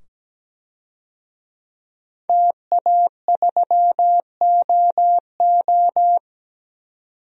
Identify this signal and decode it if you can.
Morse code, heard as a keyed tone, about 17 wpm, 700 Hz. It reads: TA3OO
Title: DXCC-100 Call Signs - 5 Characters - 17wpm